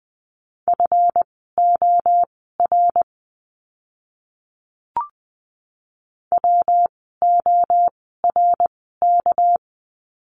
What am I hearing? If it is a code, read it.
Morse code, heard as a keyed tone, about 20 wpm, spaced as for 15 wpm, 700 Hz. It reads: FOR WORK
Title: 237 Lesson 38f - Letter q - Words 20wpm